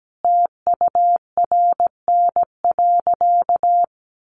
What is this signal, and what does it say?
Morse code, heard as a keyed tone, about 17 wpm, 700 Hz. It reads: TURN.
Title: Sentences from Top 700 Words 17wpm